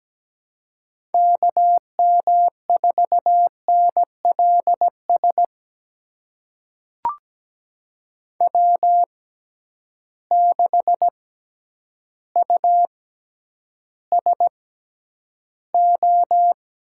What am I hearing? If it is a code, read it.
Morse code, heard as a keyed tone, about 17 wpm, spaced as for 9 wpm, 700 Hz. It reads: KM4NLS W6USO
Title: USA Call Signs - 6x Character Spacing - 17wpm